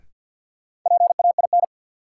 Morse code, heard as a keyed tone, about 52 wpm, 700 Hz. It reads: PAIN
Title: Sentences from Top 1600 Words - New Word Drill Alt - 50wpm